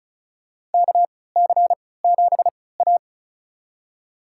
Morse code, heard as a keyed tone, about 34 wpm, spaced as for 20 wpm, 700 Hz. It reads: KC7A
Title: USA Call Signs - 4 Characters - 3x Character Spacing - 35wpm